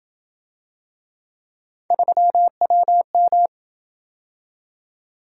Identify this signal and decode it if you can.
Morse code, heard as a keyed tone, about 27 wpm, 700 Hz. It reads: 3WM